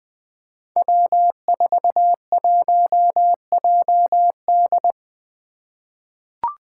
Morse code, heard as a keyed tone, about 21 wpm, 700 Hz. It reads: W41JD
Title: Five Challenging Letter-Number Combinations 20wpm